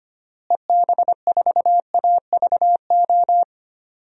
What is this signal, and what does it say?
Morse code, heard as a keyed tone, about 25 wpm, 690 Hz. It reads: EB4AVO
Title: DXCC-100 Call Signs - 6 Characters - Repeated 3x - 25wpm